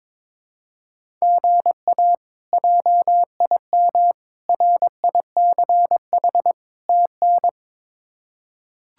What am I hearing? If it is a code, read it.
Morse code, heard as a keyed tone, about 22 wpm, 700 Hz. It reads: GA JIM RICH TN
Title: K1USN SST QSO Practice - 22wpm